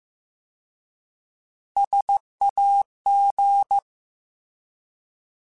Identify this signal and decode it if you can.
Morse code, heard as a keyed tone, about 15 wpm, 800 Hz. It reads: SAG